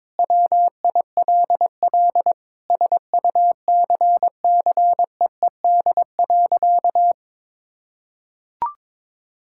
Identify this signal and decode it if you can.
Morse code, heard as a keyed tone, about 22 wpm, 700 Hz. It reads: WILL SUCCEED.
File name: practice / 22wpm / Sentences with 8-Word Phrases - 22wpm